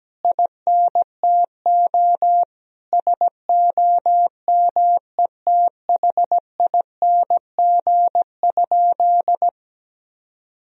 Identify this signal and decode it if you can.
Morse code, heard as a keyed tone, about 17 wpm, 700 Hz. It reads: INTO SOMETHING?
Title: Sentences from Top 1000 Words - Review Part 2 - 17wpm